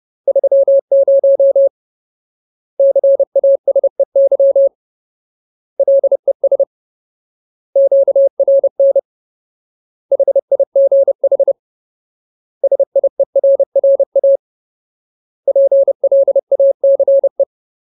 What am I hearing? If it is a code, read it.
Morse code, heard as a keyed tone, about 30 wpm, 550 Hz. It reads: 30 CASEY LES QRN HIGH SIERRA PLACE
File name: Ultimate Word List / 30WPM Ultimate